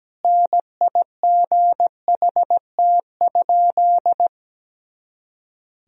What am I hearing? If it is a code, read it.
Morse code, heard as a keyed tone, about 17 wpm, 700 Hz. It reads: NIGHT?
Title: Sentences from Top 300 Words and Abbreviations 17wpm